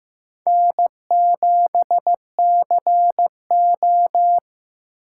Morse code, heard as a keyed tone, about 15 wpm, 700 Hz. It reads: N7CO
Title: USA Call Signs - 4 Characters - 3x Character Spacing - 15wpm